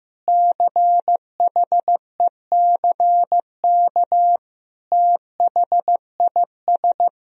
Morse code, heard as a keyed tone, about 15 wpm, 700 Hz. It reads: CHECK THIS
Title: Sentences from Top 1000 Words - Review Part 3 - 15wpm